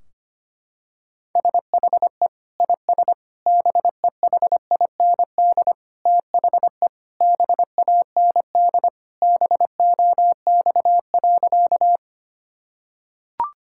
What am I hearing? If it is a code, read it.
Morse code, heard as a keyed tone, about 25 wpm, 700 Hz. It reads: SHE IS BEHIND THE BAND BOX.
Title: Sentences from Top 1600 Words 25wpm